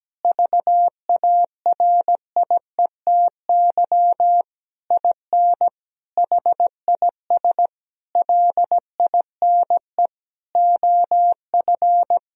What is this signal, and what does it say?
Morse code, heard as a keyed tone, about 17 wpm, 700 Hz. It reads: VARIETY IN HIS LINE OF
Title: Sentences from Top 1000 Words - Review Part 4 - 17wpm